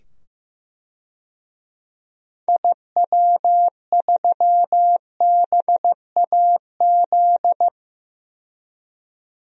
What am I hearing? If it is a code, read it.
Morse code, heard as a keyed tone, about 15 wpm, 700 Hz. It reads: IW3BAZ
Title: DXCC-100 Call Signs - 6 Characters - 15wpm